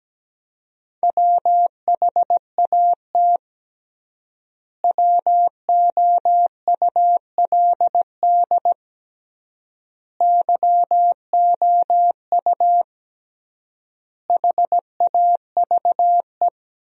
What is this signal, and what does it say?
Morse code, heard as a keyed tone, about 17 wpm, 700 Hz. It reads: WHAT WOULD YOU HAVE
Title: 3x Word-Spacing - Sentences from Top 200 Words - 17wpm